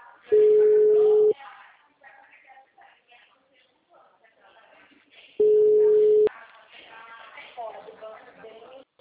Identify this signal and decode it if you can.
Ringback tone (Europe, 425 Hz), with speech.
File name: out- -2014- - - .80810